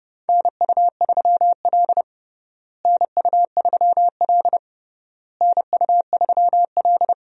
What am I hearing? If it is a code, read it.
Morse code, heard as a keyed tone, about 31 wpm, 700 Hz. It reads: NU3L NU3L NU3L
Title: USA Call Signs - 4 Characters - Repeated 3x - 30wpm